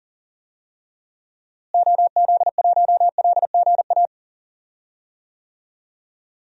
Morse code, heard as a keyed tone, about 38 wpm, 700 Hz. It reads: OZ1LGA
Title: DXCC-100 Call Signs - 6 Characters - Repeated 2x - 40wpm